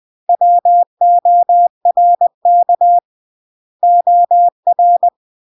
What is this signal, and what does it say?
Morse code, heard as a keyed tone, about 20 wpm, 700 Hz. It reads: WORK OR
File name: CW / jodorowsky / 82 commandments 20-20